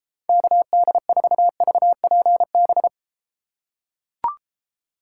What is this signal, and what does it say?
Morse code, heard as a keyed tone, about 32 wpm, 700 Hz. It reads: KD4VPB